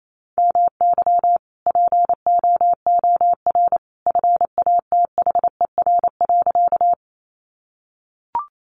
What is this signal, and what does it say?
Morse code, heard as a keyed tone, about 28 wpm, 700 Hz. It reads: MY POOR FATHER.
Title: Sentences from Top 1000 Words 28wpm